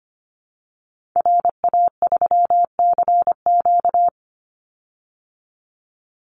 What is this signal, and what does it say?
Morse code, heard as a keyed tone, about 25 wpm, 700 Hz. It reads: RA3CQ
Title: Call Signs Repeated 3x - 25wpm